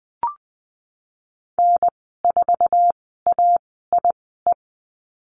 Morse code, heard as a keyed tone, about 20 wpm, spaced as for 15 wpm, 700 Hz. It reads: N4AIE
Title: 055 Lesson 10g - Letter r - Call Signs 20wpm